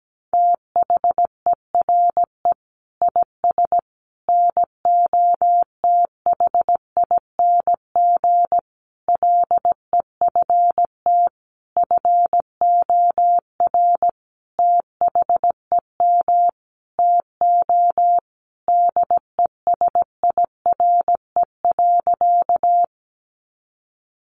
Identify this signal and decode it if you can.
Morse code, heard as a keyed tone, about 17 wpm, 700 Hz. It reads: THERE IS NOTHING LEFT FOR THEM TO DESIRE.
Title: Sentences from Top 1100 Words 17wpm